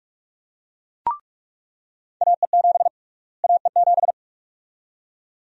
Morse code, heard as a keyed tone, about 41 wpm, 700 Hz. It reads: AE7 AE7
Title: USA Call Sign Prefixes - Repeated 2x - 45wpm